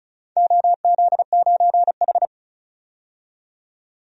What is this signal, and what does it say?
Morse code, heard as a keyed tone, about 36 wpm, 700 Hz. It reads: OZ9H
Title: DXCC-100 Call Signs - 4 Characters - 35wpm